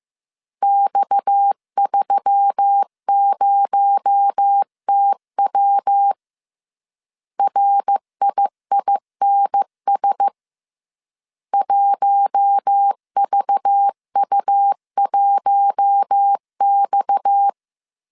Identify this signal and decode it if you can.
Morse code, heard as a keyed tone, about 15 wpm, 800 Hz. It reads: X30TW RIINS 1VU1X
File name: CW / DET / 1 6 1500C S C 14wpm 2951